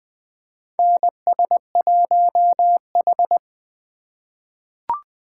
Morse code, heard as a keyed tone, about 20 wpm, 700 Hz. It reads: NS1H
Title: USA Call Signs - 3x Character Spacing - 20wpm